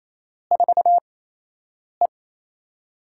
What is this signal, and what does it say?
Morse code, heard as a keyed tone, about 28 wpm, spaced as for 8 wpm, 700 Hz. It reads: 4E
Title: USA Call Signs - 6 Characters - 8x Character Spacing - 28wpm